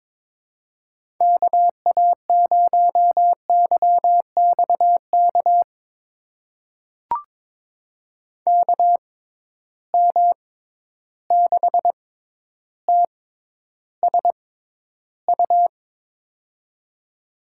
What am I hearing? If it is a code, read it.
Morse code, heard as a keyed tone, about 22 wpm, spaced as for 11 wpm, 700 Hz. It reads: KA0YXK KM6TSU
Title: USA Call Signs - 6 Characters - 6x Character Spacing - 22wpm